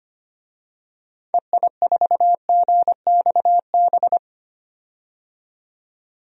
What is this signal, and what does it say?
Morse code, heard as a keyed tone, about 25 wpm, 700 Hz. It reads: EI4GXB
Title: DXCC-100 Call Signs - 6 Characters - 25wpm